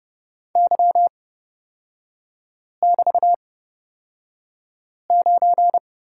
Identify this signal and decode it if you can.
Morse code, heard as a keyed tone, about 30 wpm, spaced as for 5 wpm, 700 Hz. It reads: Y<BT>9
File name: Lesson 36e - BT - Three Characters 30wpm